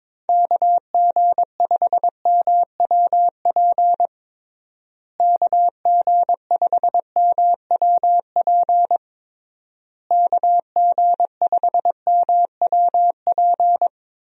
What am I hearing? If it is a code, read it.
Morse code, heard as a keyed tone, about 22 wpm, 700 Hz. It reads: KG5MWP KG5MWP KG5MWP
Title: USA Call Signs - 6 Characters - Repeated 3x - 22wpm